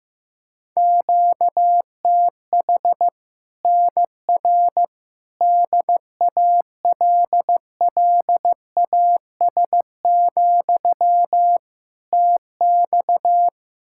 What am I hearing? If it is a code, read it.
Morse code, heard as a keyed tone, about 15 wpm, 700 Hz. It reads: QTH NR DALLAS, TX
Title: Practice QSO - Exchange 1 - Beginning - 15wpm